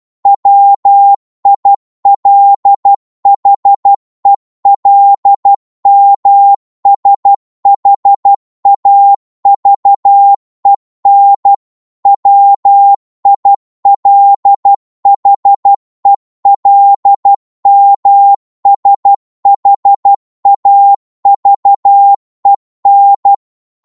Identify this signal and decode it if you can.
Morse code, heard as a keyed tone, about 12 wpm, 800 Hz. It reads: WILHELMSHAVEN WILHELMSHAVEN
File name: Morse / qso-12wpm